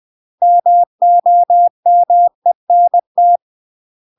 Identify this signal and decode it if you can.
Morse code, heard as a keyed tone, about 20 wpm, 700 Hz. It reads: MOMENT
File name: UK MASTER QSOs TRAINING / 20WPM / UK MASTER QSOs 0026